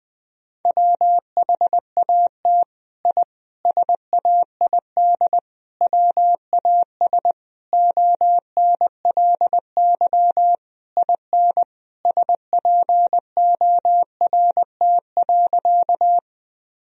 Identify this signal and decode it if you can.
Morse code, heard as a keyed tone, about 20 wpm, 700 Hz. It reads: WHAT I SAID WAS ONLY IN SPORT.
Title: Sentences from Top 1000 Words - Review Part 4 - 20wpm